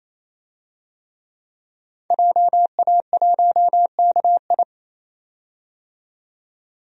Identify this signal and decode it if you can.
Morse code, heard as a keyed tone, about 28 wpm, 700 Hz. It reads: JA1KI
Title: DXCC-100 Call Signs - 5 Characters - 28wpm